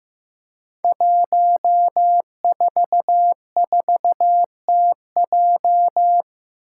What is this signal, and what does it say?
Morse code, heard as a keyed tone, about 15 wpm, 700 Hz. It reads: 144TJ
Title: Five Challenging Letter-Number Combinations 15wpm